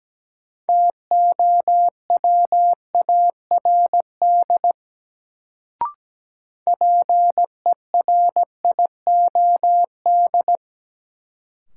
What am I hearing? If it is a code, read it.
Morse code, heard as a keyed tone, about 17 wpm, 700 Hz. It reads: TOWARD PERIOD
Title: Sets of 5 Words - New Word Drill - 17wpm